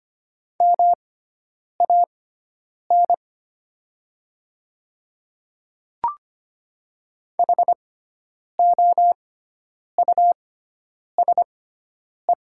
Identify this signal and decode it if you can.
Morse code, heard as a keyed tone, about 25 wpm, spaced as for 9 wpm, 690 Hz. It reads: MAN HOUSE
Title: 6x Character Spacing - Top 200 Words - 25wpm